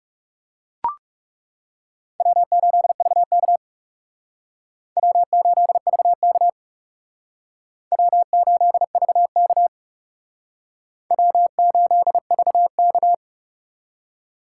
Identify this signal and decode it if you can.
Morse code, heard as a keyed tone, about 37 wpm, 700 Hz. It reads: W8VK W8VK W8VK W8VK
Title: Speed Racing - USA 1x2 Call Signs 30wpm